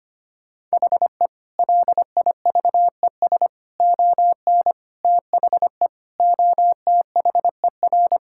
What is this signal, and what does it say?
Morse code, heard as a keyed tone, about 25 wpm, 700 Hz. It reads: HE LIVES ON THE OTHER
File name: practice / 25wpm / Sentences with 4-Word Phrases - 25wpm